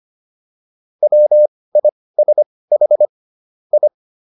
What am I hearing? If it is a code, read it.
Morse code, heard as a keyed tone, about 25 wpm, spaced as for 18 wpm, 600 Hz. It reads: WISH I